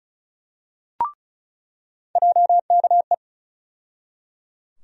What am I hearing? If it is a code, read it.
Morse code, heard as a keyed tone, about 35 wpm, 700 Hz. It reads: JKE
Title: USA Call Sign Postfixes - 3 Letters 35wpm